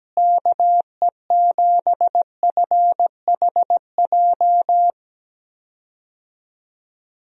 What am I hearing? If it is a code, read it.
Morse code, heard as a keyed tone, about 17 wpm, 700 Hz. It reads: KE7FHJ
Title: USA Call Signs - 6 Characters - Repeated 2x - 17wpm